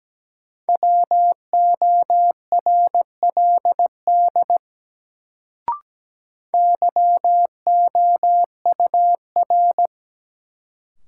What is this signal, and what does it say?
Morse code, heard as a keyed tone, about 17 wpm, 700 Hz. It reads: WORLD YOUR
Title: Sets of 2 Words - New Word Drill - 17wpm